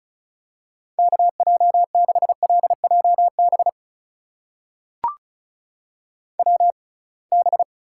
Morse code, heard as a keyed tone, about 36 wpm, spaced as for 27 wpm, 700 Hz. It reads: KJ6LJB WB
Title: USA Call Signs - 6 Characters - 6x Character Spacing - 35wpm